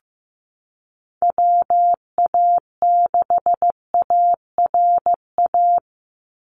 Morse code, heard as a keyed tone, about 15 wpm, 700 Hz. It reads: WA6ARA
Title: NA SOTA Chaser Callsigns 15wpm